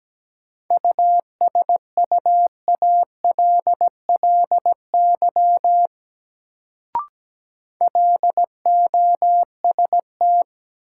Morse code, heard as a keyed tone, about 17 wpm, 700 Hz. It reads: USUALLY LOST
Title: Sentences from Top 1400 Words - New Word Drill Alt - 17wpm